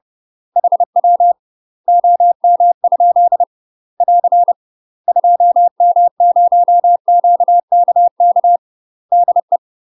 Morse code, heard as a keyed tone, about 30 wpm, 700 Hz. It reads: HW OM? <AR> 2M0QKK DE
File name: UK MASTER QSOs TRAINING / 30WPM / UK MASTER QSOs 0038